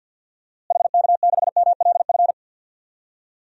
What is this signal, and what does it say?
Morse code, heard as a keyed tone, about 49 wpm, 700 Hz. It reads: HK6MLF
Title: DXCC-100 Call Signs - 6 Characters - Repeated 2x - 50wpm